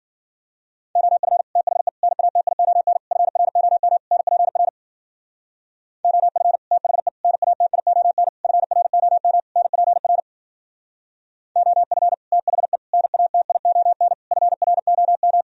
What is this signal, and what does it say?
Morse code, heard as a keyed tone, about 51 wpm, 700 Hz. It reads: OF THE NATION FROM NPR OF THE NATION FROM NPR OF THE NATION FROM